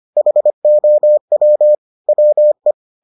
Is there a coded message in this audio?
Yes: Morse code from 0.2 to 2.7 s, heard as a keyed tone, about 25 wpm, 600 Hz. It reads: HOW WE